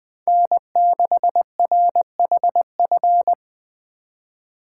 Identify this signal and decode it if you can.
Morse code, heard as a keyed tone, about 20 wpm, 700 Hz. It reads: N6RHF
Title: Speed Racing - USA 1x3 Call Signs 17wpm